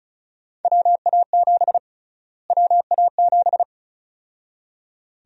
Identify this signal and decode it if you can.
Morse code, heard as a keyed tone, about 36 wpm, 700 Hz. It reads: WA7 WA7
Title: USA Call Sign Prefixes - Repeated 2x - 35wpm